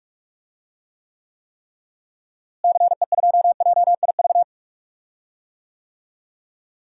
Morse code, heard as a keyed tone, about 41 wpm, 700 Hz. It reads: CE2JIV
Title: DXCC-100 Call Signs - 6 Characters - Repeated 3x - 45wpm